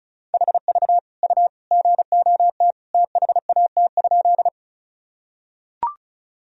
Morse code, heard as a keyed tone, about 36 wpm, 700 Hz. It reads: HV U GOT THAT?
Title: Sentences from Top 300 Words and Abbreviations 35wpm